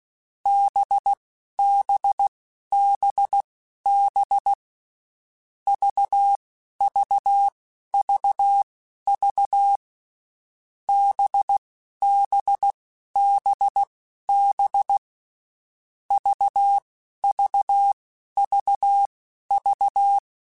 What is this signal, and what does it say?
Morse code, heard as a keyed tone, about 16 wpm, spaced as for 11 wpm, 800 Hz. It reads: BBBB VVVV BBBB VVVV
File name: Groups@10